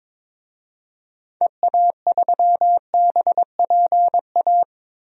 Morse code, heard as a keyed tone, about 22 wpm, 700 Hz. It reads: EA3BPA
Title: DXCC-100 Call Signs - 6 Characters - Repeated 2x - 22wpm